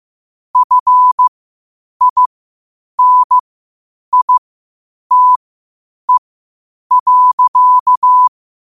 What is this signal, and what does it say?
Morse code, heard as a keyed tone, about 15 wpm, spaced as for 8 wpm, 1000 Hz. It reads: FINITE.
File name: qotdfiles / 8wpm